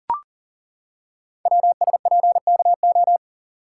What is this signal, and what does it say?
Morse code, heard as a keyed tone, about 39 wpm, 700 Hz. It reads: WSPKO
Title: Code Groups - 5 Letters - 40wpm